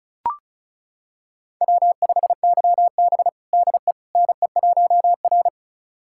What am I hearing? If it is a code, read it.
Morse code, heard as a keyed tone, about 36 wpm, 700 Hz. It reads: W5YB DE NE1R
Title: USA Callsign DE Callsign - 4 Characters - 35wpm